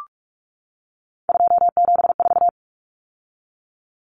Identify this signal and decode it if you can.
Morse code, heard as a keyed tone, about 44 wpm, 700 Hz. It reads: 274
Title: Numbers - 3-Digit 45wpm